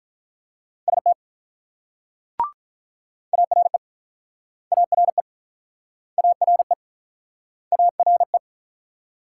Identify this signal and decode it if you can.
Morse code, heard as a keyed tone, about 47 wpm, 700 Hz. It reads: ST ARE ARE ARE ARE